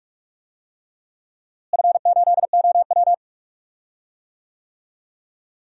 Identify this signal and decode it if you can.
Morse code, heard as a keyed tone, about 44 wpm, 700 Hz. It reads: F8OW
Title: DXCC-100 Call Signs - 4 Characters - Repeated 3x - 45wpm